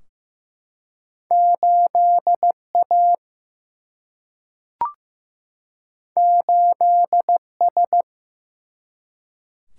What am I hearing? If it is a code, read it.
Morse code, heard as a keyed tone, about 15 wpm, 700 Hz. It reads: 8A 8S
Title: Single-Letter Single-Number 15wpm